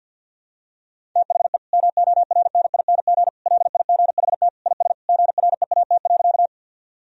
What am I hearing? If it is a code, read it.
Morse code, heard as a keyed tone, about 51 wpm, 700 Hz. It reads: THE MORNING LIGHT IS GREAT.
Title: Sentences from Top 1000 Words - Review Part 1 - 50wpm